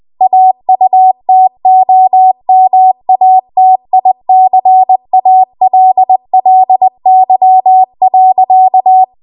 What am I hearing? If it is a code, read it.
Morse code, heard as a keyed tone, about 20 wpm, 750 Hz. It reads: AUTOMATICALLY.
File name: Archive / 20 WPM / 20WPM